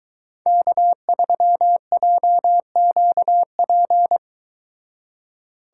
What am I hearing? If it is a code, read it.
Morse code, heard as a keyed tone, about 23 wpm, 700 Hz. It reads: K3JQP